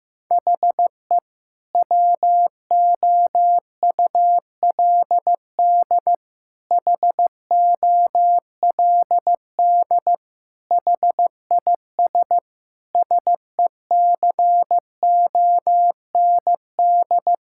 Morse code, heard as a keyed tone, about 15 wpm, 700 Hz. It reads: HE WOULD HOLD HIS SECOND
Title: Sentences from Top 1000 Words - Review Part 6 - 15wpm